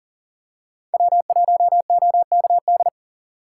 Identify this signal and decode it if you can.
Morse code, heard as a keyed tone, about 40 wpm, 700 Hz. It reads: W1OKD
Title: USA Call Signs - 5 Characters - 2x Character Spacing - 40wpm